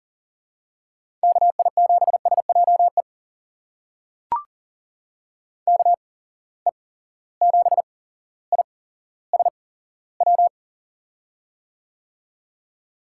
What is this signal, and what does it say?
Morse code, heard as a keyed tone, about 40 wpm, spaced as for 17 wpm, 700 Hz. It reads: KI7SJE KE7ISW
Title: USA Call Signs - 6 Characters - 8x Character Spacing - 40wpm